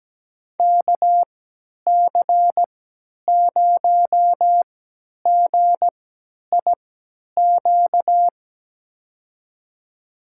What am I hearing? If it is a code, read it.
Morse code, heard as a keyed tone, about 17 wpm, spaced as for 10 wpm, 700 Hz. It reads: KC0GIQ
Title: USA Call Signs - 6 Characters - 3x Character Spacing - 17wpm